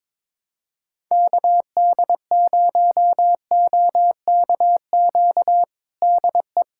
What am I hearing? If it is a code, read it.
Morse code, heard as a keyed tone, about 21 wpm, 700 Hz. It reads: KD0OKQ DE
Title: USA Callsign DE Callsign - 6 Characters - 22wpm